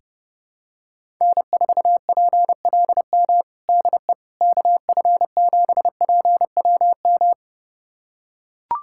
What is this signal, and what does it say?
Morse code, heard as a keyed tone, about 28 wpm, 700 Hz. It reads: N4PLM DE KF7PWM
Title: USA Callsign DE Callsign 30wpm